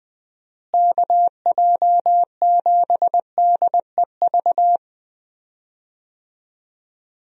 Morse code, heard as a keyed tone, about 21 wpm, 700 Hz. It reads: KJ7DEV
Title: USA Call Signs - 6 Characters - Repeated 2x - 20wpm